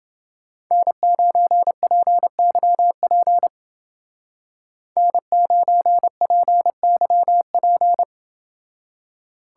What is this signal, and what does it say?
Morse code, heard as a keyed tone, about 28 wpm, 700 Hz. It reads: N9PYP N9PYP